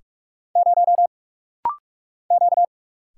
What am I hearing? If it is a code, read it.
Morse code, heard as a keyed tone, about 46 wpm, 700 Hz. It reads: 0 Q
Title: Single Letter-Number - Rapid-Fire 45wpm